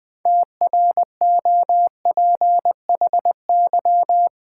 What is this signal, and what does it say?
Morse code, heard as a keyed tone, about 20 wpm, 700 Hz. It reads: TROPHY
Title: 6 Letter Words - First 3 Letters Given - 20wpm